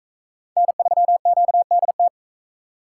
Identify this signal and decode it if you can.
Morse code, heard as a keyed tone, about 41 wpm, 700 Hz. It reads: N3QDT